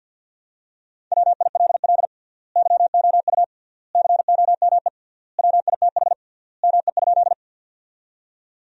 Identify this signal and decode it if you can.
Morse code, heard as a keyed tone, about 51 wpm, 700 Hz. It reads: WILL YOU COME WITH ME?
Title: 3x Word-Spacing - Sentences from Top 200 Words - 50wpm